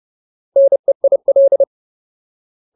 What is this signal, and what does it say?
Morse code, heard as a keyed tone, about 30 wpm, 550 Hz. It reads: NEIL
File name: Ultimate Word List / 30WPM Ultimate